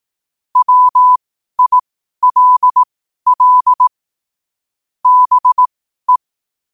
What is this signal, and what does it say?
Morse code, heard as a keyed tone, about 18 wpm, spaced as for 12 wpm, 1000 Hz. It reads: WILL BE
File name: qotdfiles / 12wpm